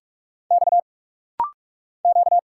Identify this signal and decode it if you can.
Morse code, heard as a keyed tone, about 46 wpm, 690 Hz. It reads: X Q
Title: Single Letters - Rapid-Fire 45wpm